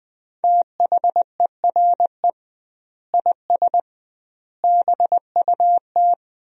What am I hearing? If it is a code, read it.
Morse code, heard as a keyed tone, about 21 wpm, 700 Hz. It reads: THERE IS BUT
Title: 2x Word-Spacing - Sentences from Top 100 Words - 20wpm